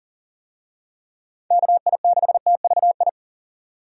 Morse code, heard as a keyed tone, about 40 wpm, 700 Hz. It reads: KI6TVI
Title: USA Call Signs - 6 Characters - 40wpm